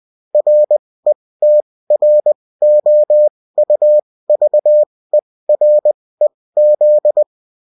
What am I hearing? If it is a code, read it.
Morse code, heard as a keyed tone, about 20 wpm, spaced as for 16 wpm, 600 Hz. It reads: RETROUVEREZ